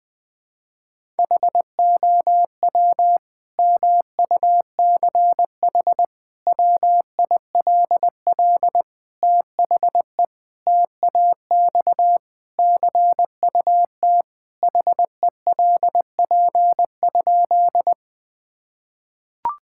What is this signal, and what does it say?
Morse code, heard as a keyed tone, about 20 wpm, 700 Hz. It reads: HOW MUCH WILL THE TAX CUT HELP?
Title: Sentences from Top 1000 Words - Review Part 5 - 20wpm